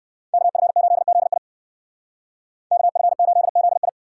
Morse code, heard as a keyed tone, about 54 wpm, 700 Hz. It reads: KF8ZI KF8ZI
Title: Speed Racing - USA Call Signs 45wpm